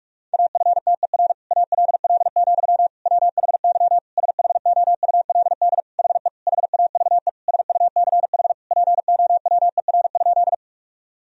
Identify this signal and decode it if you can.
Morse code, heard as a keyed tone, about 46 wpm, 700 Hz. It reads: AFTER ALL, WHY SHOULD HE HAVE SUCH POWER?
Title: Sentences from Top 1000 Words - Review Part 2 - 45wpm